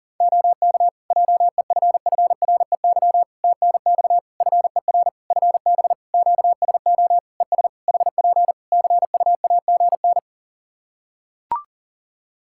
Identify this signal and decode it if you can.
Morse code, heard as a keyed tone, about 37 wpm, 700 Hz. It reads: OK JEFFREY TNX FER FB QSO ES HP CUAGN
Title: Practice QSO - Exchange 4 - Ending - 40wpm